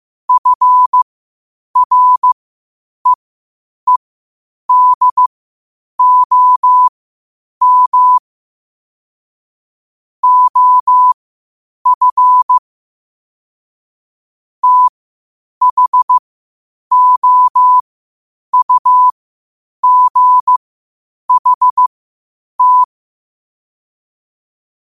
Morse code, heard as a keyed tone, about 15 wpm, spaced as for 8 wpm, 1000 Hz. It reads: FREEDOM OF THOUGHT
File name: qotdfiles / 8wpm